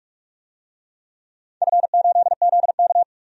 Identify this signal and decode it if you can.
Morse code, heard as a keyed tone, about 42 wpm, 700 Hz. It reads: F8ZK
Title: DXCC-100 Call Signs - 4 Characters - 45wpm